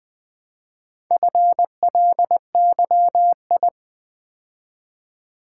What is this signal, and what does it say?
Morse code, heard as a keyed tone, about 20 wpm, 700 Hz. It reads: FLYI
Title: Build A Word - 2 Letters At A Time - 6-Letter Words - 20wpm